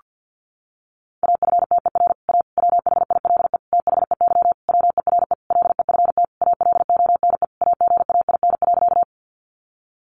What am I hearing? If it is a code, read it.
Morse code, heard as a keyed tone, about 50 wpm, 700 Hz. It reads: AFTER A WHILE THEY WERE LEFT ALONE AGAIN.
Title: Sentences from Top 1000 Words 50wpm